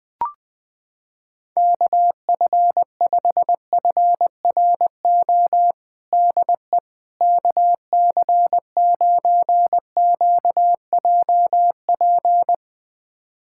Morse code, heard as a keyed tone, about 20 wpm, 700 Hz. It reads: KF5FRO DE KC9QJP
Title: USA Callsign DE Callsign - 6 Characters - 20wpm